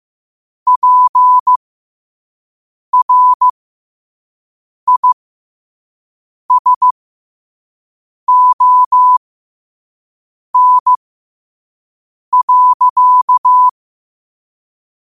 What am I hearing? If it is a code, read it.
Morse code, heard as a keyed tone, about 15 wpm, spaced as for 5 wpm, 1000 Hz. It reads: PRISON.
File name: qotdfiles / 5wpm